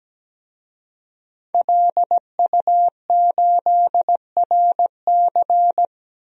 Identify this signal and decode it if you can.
Morse code, heard as a keyed tone, about 17 wpm, 700 Hz. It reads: LU8RC
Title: DXCC-100 Call Signs - 5 Characters - 17wpm